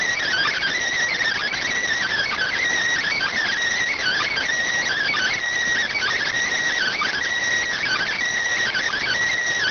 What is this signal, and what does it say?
Emergency Alert System (SAME) tones (header bursts), heard as harsh buzzing bursts. It is heard from 0.2 to 9.2 s.